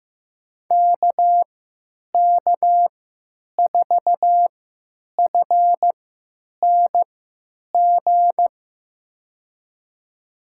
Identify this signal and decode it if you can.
Morse code, heard as a keyed tone, about 15 wpm, spaced as for 9 wpm, 700 Hz. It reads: KK4FNG
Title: USA Call Signs - 6 Characters - 3x Character Spacing - 15wpm